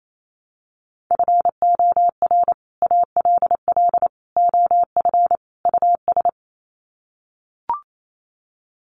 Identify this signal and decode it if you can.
Morse code, heard as a keyed tone, about 28 wpm, 700 Hz. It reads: FOR ALL OF US
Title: Sets of 4 Words 28wpm